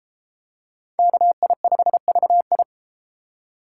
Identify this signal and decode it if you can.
Morse code, heard as a keyed tone, about 32 wpm, 700 Hz. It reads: KI5VI